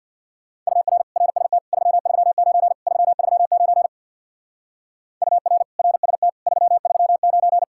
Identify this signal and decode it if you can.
Morse code, heard as a keyed tone, about 66 wpm, 700 Hz. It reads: UR RST 339 339 UR RST 339